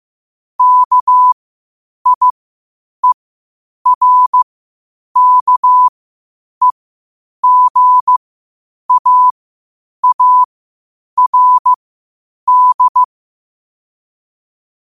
Morse code, heard as a keyed tone, about 15 wpm, spaced as for 8 wpm, 1000 Hz. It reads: KIERKEGAARD
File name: qotdfiles / 8wpm